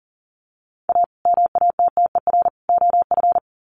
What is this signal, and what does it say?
Morse code, heard as a keyed tone, about 40 wpm, 700 Hz. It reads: A MATTER OF
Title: Sets of 3 Words 40wpm